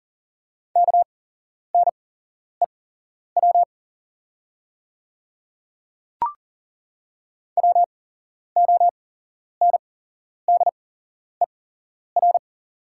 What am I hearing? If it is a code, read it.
Morse code, heard as a keyed tone, about 40 wpm, spaced as for 11 wpm, 700 Hz. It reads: KNEW WONDER
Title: 8x Character Spacing - Top 500 Words - 40wpm